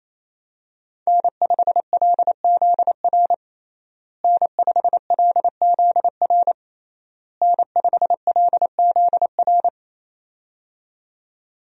Morse code, heard as a keyed tone, about 28 wpm, 700 Hz. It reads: N5LZR N5LZR N5LZR
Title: USA Call Signs - 5 Characters - Repeated 3x - 28wpm